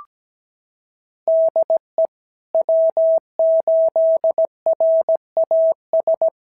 Morse code, heard as a keyed tone, about 17 wpm, 650 Hz. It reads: DE W8RAS
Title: SOTA QSO Practice 17wpm